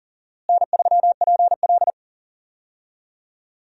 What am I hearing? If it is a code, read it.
Morse code, heard as a keyed tone, about 37 wpm, 700 Hz. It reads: N3PL
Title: USA Call Signs - 4 Characters - Repeated 3x - 40wpm